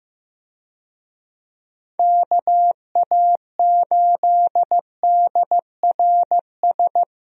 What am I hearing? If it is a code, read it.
Morse code, heard as a keyed tone, about 15 wpm, 700 Hz. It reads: KA8DRS
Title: USA Call Signs - 6 Characters - Repeated 2x - 15wpm